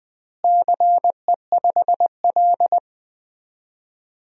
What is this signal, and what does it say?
Morse code, heard as a keyed tone, about 20 wpm, 700 Hz. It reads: CE5L
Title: DXCC-100 Call Signs - 4 Characters - Repeated 3x - 20wpm